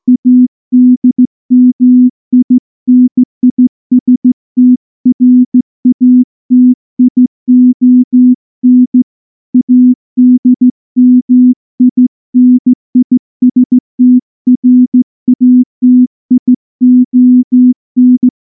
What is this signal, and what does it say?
Morse code, heard as a keyed tone, about 15 wpm, 250 Hz. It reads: ADMINISTRATION ADMINISTRATION
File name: morse / morse-training-15WPM